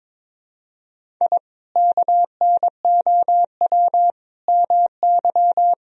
Morse code, heard as a keyed tone, about 22 wpm, 700 Hz. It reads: I KNOW MY
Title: Sentences from Top 1000 Words - Review Part 1 - 22wpm